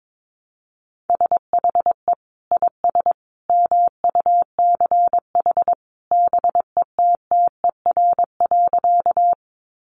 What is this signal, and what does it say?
Morse code, heard as a keyed tone, about 22 wpm, 700 Hz. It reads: SHE IS MUCH BETTER.
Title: Sentences from Top 400 Words 22wpm